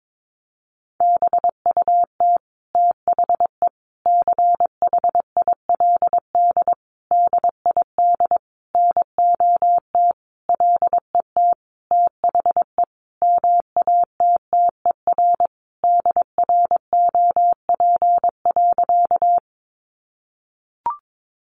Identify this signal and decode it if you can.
Morse code, heard as a keyed tone, about 22 wpm, 700 Hz. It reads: BUT THE CHILD DID NOT LET THE MATTER DROP.
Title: Sentences from Top 1400 Words 22wpm